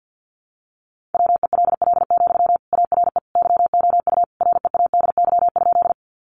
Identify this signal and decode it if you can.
Morse code, heard as a keyed tone, about 50 wpm, 700 Hz. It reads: WELL, ARE YOU READY?
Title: Sentences from Top 400 Words 50wpm